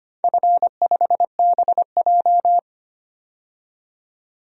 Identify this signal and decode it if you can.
Morse code, heard as a keyed tone, about 25 wpm, 700 Hz. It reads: F5BJ
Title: DXCC-100 Call Signs - 4 Characters - 25wpm